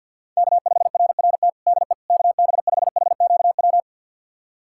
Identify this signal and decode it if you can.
Morse code, heard as a keyed tone, about 51 wpm, 700 Hz. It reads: K5RRT DE KB5HQW
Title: USA Callsign DE Callsign 50wpm